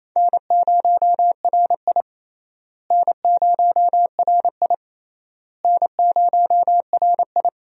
Morse code, heard as a keyed tone, about 28 wpm, 700 Hz. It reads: N0RI N0RI N0RI
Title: USA Call Signs - 4 Characters - Repeated 3x - 28wpm